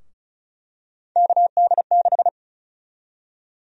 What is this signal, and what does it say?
Morse code, heard as a keyed tone, about 36 wpm, 700 Hz. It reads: KD6
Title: USA Call Sign Prefixes - 35wpm